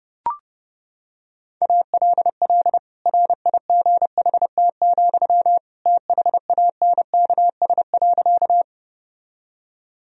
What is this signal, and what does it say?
Morse code, heard as a keyed tone, about 32 wpm, 700 Hz. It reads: ALL RIGHT, THANKS.
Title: Sentences from Top 1000 Words - Review Part 3 - 30wpm